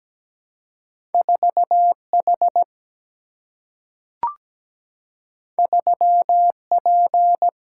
Morse code, heard as a keyed tone, about 17 wpm, 700 Hz. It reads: 4 H 3 P
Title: Single-Letter Single-Number 17wpm